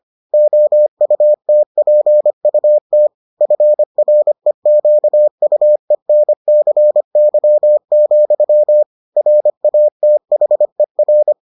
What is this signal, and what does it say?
Morse code, heard as a keyed tone, about 25 wpm, 600 Hz. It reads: OUTPUT FREQUENCY, RATHER